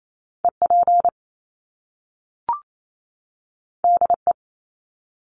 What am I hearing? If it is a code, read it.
Morse code, heard as a keyed tone, about 28 wpm, 700 Hz. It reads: E P D E
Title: Top Two Letter Combinations in Words 28wpm